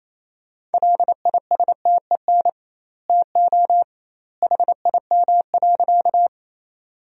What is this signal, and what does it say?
Morse code, heard as a keyed tone, about 28 wpm, 700 Hz. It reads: LISTEN TO HIM.
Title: 2x Word-Spacing - Sentences from Top 400 Words - 28wpm